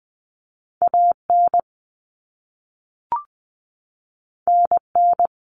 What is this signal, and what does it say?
Morse code, heard as a keyed tone, about 20 wpm, 700 Hz. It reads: AN NN
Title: Common Cut Numbers - 2-Digit 20wpm